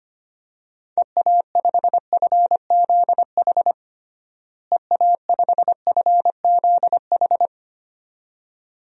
Morse code, heard as a keyed tone, about 25 wpm, 700 Hz. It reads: EA5FZH EA5FZH
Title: DXCC-100 Call Signs - 6 Characters - Repeated 2x - 25wpm